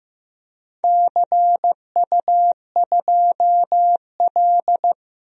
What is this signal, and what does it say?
Morse code, heard as a keyed tone, about 15 wpm, 700 Hz. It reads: CU2L
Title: DXCC-100 Call Signs - 4 Characters - Repeated 3x - 15wpm